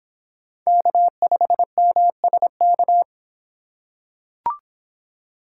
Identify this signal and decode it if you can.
Morse code, heard as a keyed tone, about 24 wpm, 700 Hz. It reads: K5MSK